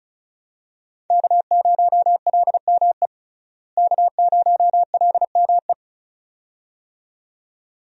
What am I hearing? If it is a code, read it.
Morse code, heard as a keyed tone, about 36 wpm, 700 Hz. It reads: K0LME K0LME
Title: USA Call Signs - 5 Characters - Repeated 2x - 35wpm